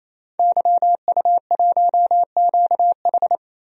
Morse code, heard as a keyed tone, about 28 wpm, 700 Hz. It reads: YU1QH
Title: DXCC-100 Call Signs - 5 Characters - 28wpm